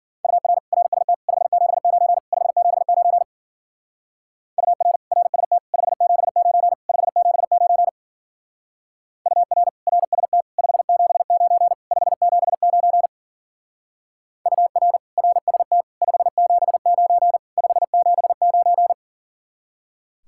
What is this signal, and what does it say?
Morse code, heard as a keyed tone, about 48 wpm, 700 Hz. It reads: UR RST 579 579 UR RST 579 579 UR RST 579 579 UR RST 579 579
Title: Speed-Racing Common QSO Phrases - 40wpm